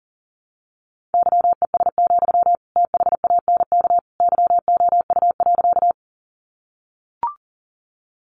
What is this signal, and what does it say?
Morse code, heard as a keyed tone, about 37 wpm, 700 Hz. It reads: YES, THANK YOU.
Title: Sentences from Top 600 Words 40wpm